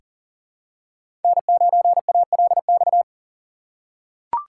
Morse code, heard as a keyed tone, about 40 wpm, 700 Hz. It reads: N9ALX
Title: USA Call Signs - 5 Characters - 4x Character Spacing - 40wpm